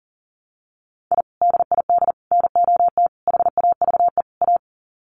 Morse code, heard as a keyed tone, about 40 wpm, 700 Hz. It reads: I DID NOT HAVE A
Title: Sets of 5 Words 40wpm